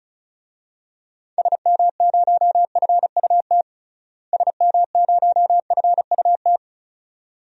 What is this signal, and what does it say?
Morse code, heard as a keyed tone, about 34 wpm, 700 Hz. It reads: SM0FUT SM0FUT
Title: DXCC-100 Call Signs - 6 Characters - Repeated 2x - 35wpm